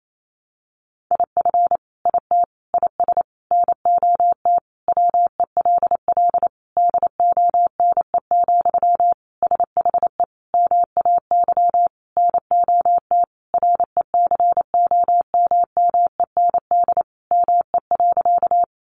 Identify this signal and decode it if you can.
Morse code, heard as a keyed tone, about 28 wpm, 700 Hz. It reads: IF IT IS NOT WELL DONE, SHE MAY NOT RECOMMEND ME.
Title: Sentences from Top 1200 Words 28wpm